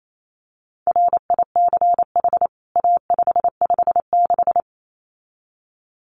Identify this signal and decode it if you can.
Morse code, heard as a keyed tone, about 28 wpm, 700 Hz. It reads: RICH A556
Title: CWT Exchange 28wpm